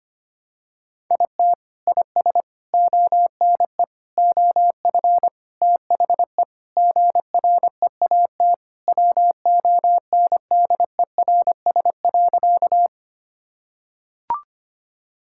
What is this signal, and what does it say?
Morse code, heard as a keyed tone, about 25 wpm, 700 Hz. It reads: IT IS ONE OF THE GREAT WONDERS.
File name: practice / 25wpm / Sentences with 4-Word Phrases - 25wpm